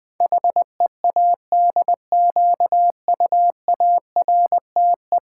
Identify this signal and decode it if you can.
Morse code, heard as a keyed tone, about 20 wpm, 700 Hz. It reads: HEADQUARTE
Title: Build A Word - 5 Letters At A Time - 12-Letter Words - 20wpm